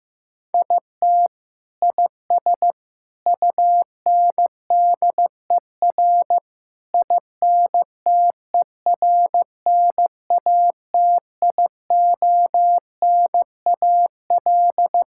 Morse code, heard as a keyed tone, about 15 wpm, 700 Hz. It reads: IT IS UNDER INTERNATIONAL
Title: Sentences from Top 1000 Words - Review Part 6 - 15wpm